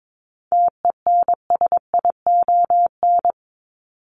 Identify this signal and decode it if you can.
Morse code, heard as a keyed tone, about 22 wpm, 700 Hz. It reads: TENSION
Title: Top 1501-1750 Words 22wpm